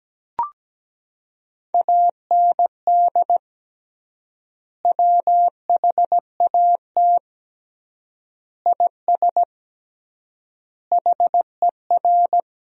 Morse code, heard as a keyed tone, about 17 wpm, 700 Hz. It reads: AND WHAT IS HER
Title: 3x Word-Spacing - Sentences from Top 200 Words - 17wpm